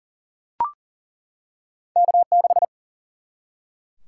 Morse code, heard as a keyed tone, about 40 wpm, 700 Hz. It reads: K6
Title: Single-Letter Single-Number 40wpm